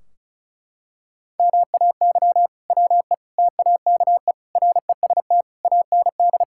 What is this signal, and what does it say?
Morse code, heard as a keyed tone, about 34 wpm, 700 Hz. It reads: MAY WE TAKE REST AND
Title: Sentences from Top 1600 Words 35wpm